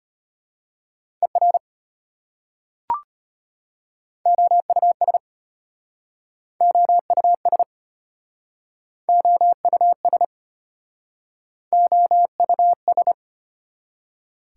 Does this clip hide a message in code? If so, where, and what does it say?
Yes: Morse code from 1.2 to 13.1 s, heard as a keyed tone, about 33 wpm, 700 Hz. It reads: ER OUS OUS OUS OUS